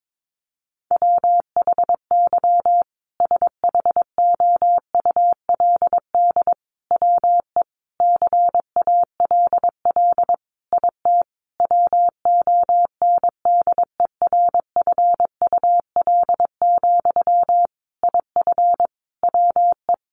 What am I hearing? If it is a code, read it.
Morse code, heard as a keyed tone, about 22 wpm, 700 Hz. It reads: WHY SHOULD WE CALL IT WONDERFUL, IF WE
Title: Sentences from Top 1100 Words 22wpm